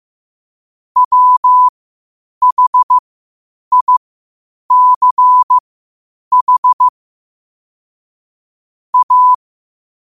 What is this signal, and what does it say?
Morse code, heard as a keyed tone, about 15 wpm, spaced as for 8 wpm, 1000 Hz. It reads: WHICH A
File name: qotdfiles / 8wpm